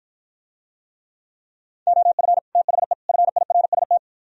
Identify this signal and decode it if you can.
Morse code, heard as a keyed tone, about 52 wpm, 700 Hz. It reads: OF THE FIRST